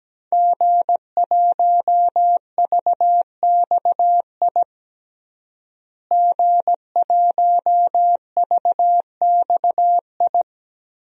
Morse code, heard as a keyed tone, about 17 wpm, 700 Hz. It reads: G1VXI G1VXI
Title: DXCC-100 Call Signs - 5 Characters - Repeated 3x - 17wpm